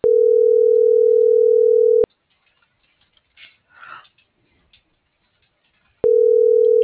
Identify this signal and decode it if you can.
Ringback tone (North America), 440 + 480 Hz.
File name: CENTRAL MULTIMODAL / out cecisandoval